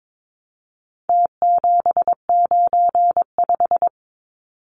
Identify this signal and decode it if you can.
Morse code, heard as a keyed tone, about 22 wpm, 700 Hz. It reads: T795
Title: Numbers - Contest Style - 4-Digit 22wpm